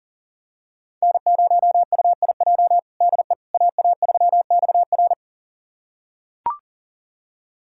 Morse code, heard as a keyed tone, about 40 wpm, 700 Hz. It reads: N0UIJ DE AA3XR
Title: USA Callsign DE Callsign - 5 Characters - 40wpm